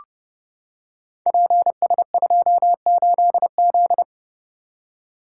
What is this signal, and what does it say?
Morse code, heard as a keyed tone, about 30 wpm, 700 Hz. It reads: PS28Z
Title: Code Groups - 5 Letter:Numbers - 30wpm